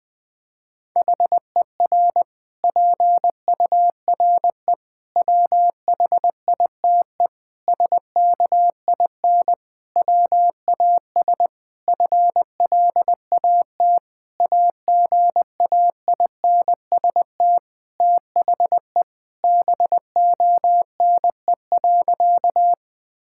Morse code, heard as a keyed tone, about 20 wpm, 700 Hz. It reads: HER PURE WHITE SKIN WAS FLAT AGAINST THE BONE.
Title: Sentences from Top 1600 Words 20wpm